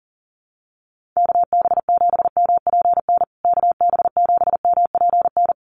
Morse code, heard as a keyed tone, about 37 wpm, 700 Hz. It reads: KB7MPN KB7MPN
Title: USA Call Signs Repeated 3x 40wpm